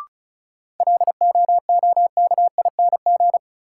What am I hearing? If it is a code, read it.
Morse code, heard as a keyed tone, about 34 wpm, 700 Hz. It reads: LOOKING
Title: Sentences from Top 1000 Words - Review Part 6 - New Word Drill - 35wpm